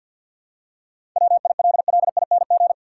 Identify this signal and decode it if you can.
Morse code, heard as a keyed tone, about 53 wpm, 700 Hz. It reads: WILLING
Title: Seven Letter Words - 50wpm